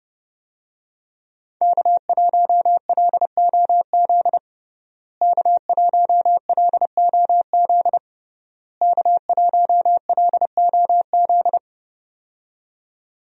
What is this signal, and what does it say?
Morse code, heard as a keyed tone, about 30 wpm, 700 Hz. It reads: K1LOZ K1LOZ K1LOZ
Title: USA Call Signs - 5 Characters - Repeated 3x - 30wpm